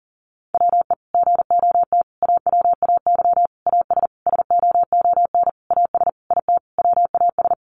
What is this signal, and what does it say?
Morse code, heard as a keyed tone, about 40 wpm, 700 Hz. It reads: WE GOT AWAY AS SOON AS IT WAS
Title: Sentences from Top 600 Words 40wpm